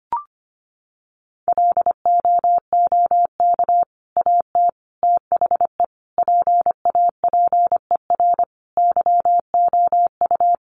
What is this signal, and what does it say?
Morse code, heard as a keyed tone, about 25 wpm, 700 Hz. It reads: LOOK AT THE PAPER YOU
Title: Sentences from Top 400 Words 25wpm